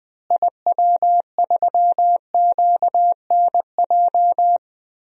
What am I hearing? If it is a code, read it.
Morse code, heard as a keyed tone, about 20 wpm, 700 Hz. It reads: IW3QNJ
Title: DXCC-100 Call Signs - 6 Characters - Repeated 3x - 20wpm